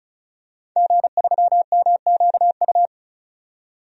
Morse code, heard as a keyed tone, about 34 wpm, 700 Hz. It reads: G3MQU
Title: DXCC-100 Call Signs - 5 Characters - 35wpm